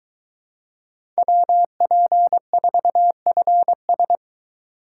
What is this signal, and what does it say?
Morse code, heard as a keyed tone, about 22 wpm, 700 Hz. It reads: WP4FS